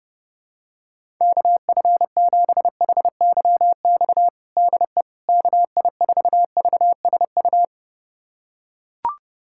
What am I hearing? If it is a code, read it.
Morse code, heard as a keyed tone, about 32 wpm, 690 Hz. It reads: KF7HYX DE KI4VSU
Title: USA Callsign DE Callsign - 6 Characters - 30wpm